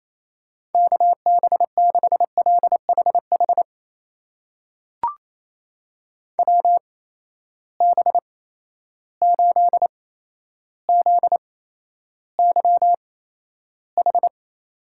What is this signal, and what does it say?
Morse code, heard as a keyed tone, about 28 wpm, spaced as for 12 wpm, 700 Hz. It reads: KB6LHH WB8ZYH
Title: USA Call Signs - 6 Characters - 8x Character Spacing - 28wpm